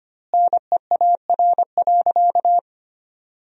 Morse code, heard as a keyed tone, about 25 wpm, 700 Hz. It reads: NEAR.
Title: 2x Word-Spacing - Sentences from Top 300 Words - 25wpm